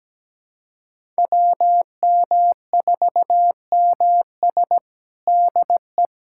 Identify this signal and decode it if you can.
Morse code, heard as a keyed tone, about 17 wpm, 700 Hz. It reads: WM4MS DE
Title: USA Callsign DE Callsign 17wpm